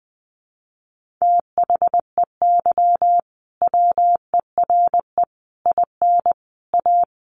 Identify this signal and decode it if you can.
Morse code, heard as a keyed tone, about 19 wpm, 700 Hz. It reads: THEY WERE IN A
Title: Sentences from Top 1400 Words 20wpm